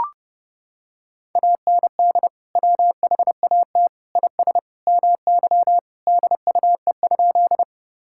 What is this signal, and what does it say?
Morse code, heard as a keyed tone, about 30 wpm, 700 Hz. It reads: AND WHAT IS MY DUE?
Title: Sentences from Top 1000 Words - Review Part 6 - 30wpm